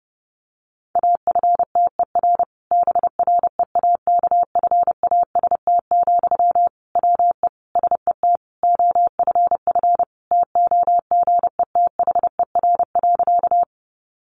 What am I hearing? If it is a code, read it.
Morse code, heard as a keyed tone, about 30 wpm, 700 Hz. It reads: AFTER BREAKFAST, WE SET OFF TOGETHER.
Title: Sentences from Top 1500 Words 30wpm